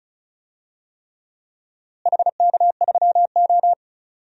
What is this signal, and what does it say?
Morse code, heard as a keyed tone, about 36 wpm, 700 Hz. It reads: HK3O
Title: DXCC-100 Call Signs - 4 Characters - Repeated 3x - 35wpm